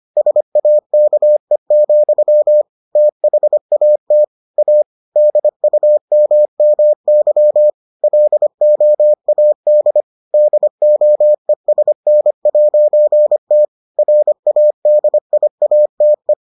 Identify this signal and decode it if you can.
Morse code, heard as a keyed tone, about 25 wpm, 600 Hz. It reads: SAKE, THAT A DUMMY LOAD DOESN'T RADIATE